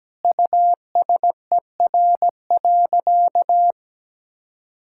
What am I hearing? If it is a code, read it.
Morse code, heard as a keyed tone, about 17 wpm, 700 Hz. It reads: USER.
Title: Sentences from Top 1000 Words - Review Part 5 - 17wpm